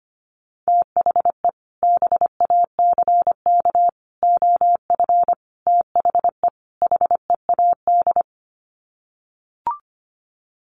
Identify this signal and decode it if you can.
Morse code, heard as a keyed tone, about 25 wpm, 700 Hz. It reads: THE BACK OF THE HEAD
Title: Sets of 5 Words Encore2 25wpm